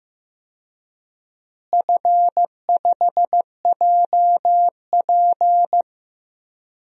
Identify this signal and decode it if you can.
Morse code, heard as a keyed tone, about 14 wpm, 700 Hz. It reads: F5JP
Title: DXCC-100 Call Signs - 4 Characters - Repeated 2x - 15wpm